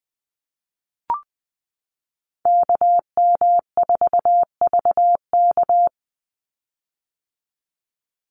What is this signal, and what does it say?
Morse code, heard as a keyed tone, about 20 wpm, 700 Hz. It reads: KM4VK
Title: USA Call Signs 20wpm